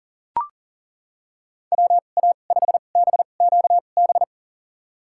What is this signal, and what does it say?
Morse code, heard as a keyed tone, about 37 wpm, spaced as for 28 wpm, 700 Hz. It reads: WA5BQB
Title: USA Call Signs - 6 Characters - 2x Character Spacing - 40wpm